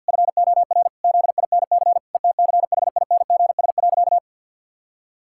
Morse code, heard as a keyed tone, about 50 wpm, 700 Hz. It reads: FOR ZINC ETCHINGS.
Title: Pangram Challenge v2 - 50wpm